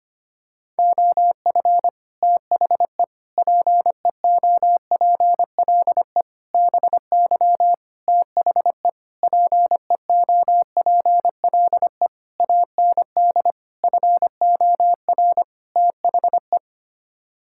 Morse code, heard as a keyed tone, about 25 wpm, 700 Hz. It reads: OF THE PEOPLE BY THE PEOPLE AND FOR THE
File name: Sets of 9 Words - 25wpm